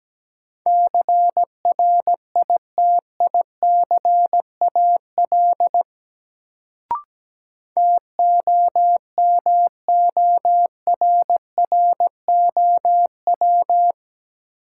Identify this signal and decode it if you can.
Morse code, heard as a keyed tone, about 17 wpm, 700 Hz. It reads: CRITICAL TOMORROW
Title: Sentences from Top 1100 Words - New Word Drill - 17wpm